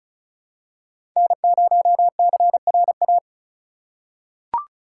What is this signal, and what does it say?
Morse code, heard as a keyed tone, about 36 wpm, 700 Hz. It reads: N0CRA
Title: USA Call Signs - 3x Character Spacing - 35wpm